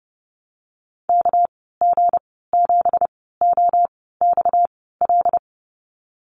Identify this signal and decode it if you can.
Morse code, heard as a keyed tone, about 30 wpm, spaced as for 17 wpm, 700 Hz. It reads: KG7OXL
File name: Lesson 37g - Letter x - Call Signs 30wpm